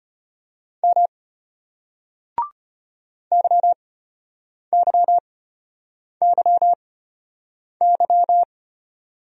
Morse code, heard as a keyed tone, about 33 wpm, spaced as for 8 wpm, 700 Hz. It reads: M YYYY